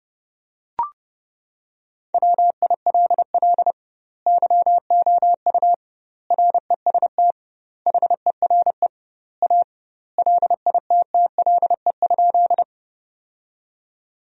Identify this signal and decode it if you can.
Morse code, heard as a keyed tone, about 30 wpm, 700 Hz. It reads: WILL YOU REST HERE A LITTLE?
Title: 2x Word-Spacing - Sentences from Top 500 Words - 30wpm